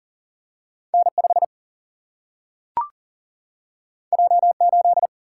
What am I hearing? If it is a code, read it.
Morse code, heard as a keyed tone, about 40 wpm, 700 Hz. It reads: N5 J8
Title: Single-Letter Single-Number 40wpm